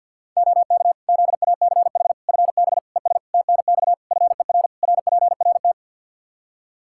Morse code, heard as a keyed tone, about 52 wpm, 700 Hz. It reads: OK ZACH FB ES TNX FER RPRT
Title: K1USN SST QSO Practice 50wpm